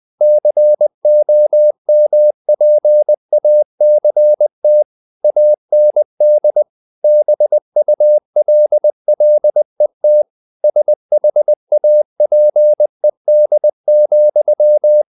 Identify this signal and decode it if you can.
Morse code, heard as a keyed tone, about 20 wpm, 600 Hz. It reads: COMPACT AND BULLET SHAPED,
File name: the old man and the sea / the old man and the sea part 1 20wpm